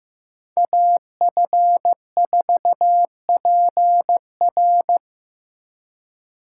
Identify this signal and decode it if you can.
Morse code, heard as a keyed tone, about 18 wpm, 700 Hz. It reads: AF4PR
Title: Speed Racing - USA Call Signs 15wpm